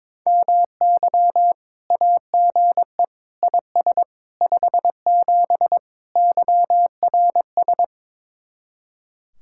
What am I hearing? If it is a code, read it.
Morse code, heard as a keyed tone, about 22 wpm, 700 Hz. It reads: MY AGE IS 57 YRS
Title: Speed-Racing Common QSO Phrases - 22wpm